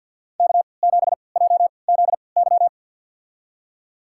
Morse code, heard as a keyed tone, about 49 wpm, spaced as for 28 wpm, 700 Hz. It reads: K7JZY
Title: USA Call Signs - 5 Characters - 3x Character Spacing - 50wpm